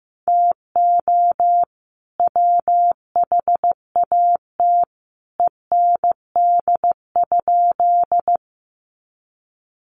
Morse code, heard as a keyed tone, about 15 wpm, 700 Hz. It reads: TO WHAT END?
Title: Sentences from Top 200 Words 15wpm